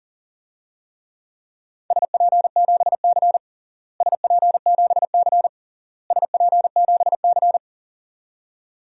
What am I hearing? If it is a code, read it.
Morse code, heard as a keyed tone, about 40 wpm, 700 Hz. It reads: SP7C SP7C SP7C
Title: DXCC-100 Call Signs - 4 Characters - Repeated 3x - 40wpm